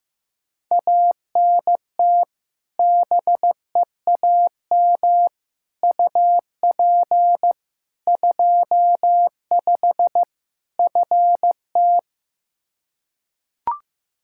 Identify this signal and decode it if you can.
Morse code, heard as a keyed tone, about 15 wpm, 700 Hz. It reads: ANT BEAM UP 25 FT
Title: K1USN SST QSO Practice 15wpm